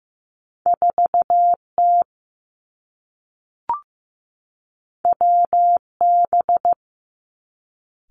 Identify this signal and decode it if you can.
Morse code, heard as a keyed tone, about 15 wpm, 700 Hz. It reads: 4 T W B
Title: Two Challenging Letter-Number Combinations 15wpm